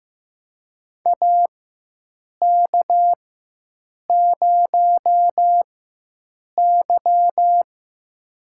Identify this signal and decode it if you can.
Morse code, heard as a keyed tone, about 15 wpm, spaced as for 7 wpm, 700 Hz. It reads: AK0Y
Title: USA Call Signs - 4 Characters - 4x Character Spacing - 15wpm